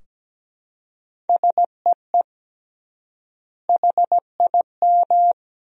Morse code, heard as a keyed tone, about 17 wpm, 700 Hz. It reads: SEE HIM
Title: 3x Word-Spacing - Sentences from Top 300 Words - 17wpm